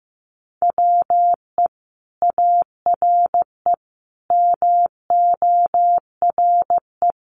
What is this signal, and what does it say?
Morse code, heard as a keyed tone, about 15 wpm, 700 Hz. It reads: WE ARE MORE
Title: Sentences from Top 500 Words 15wpm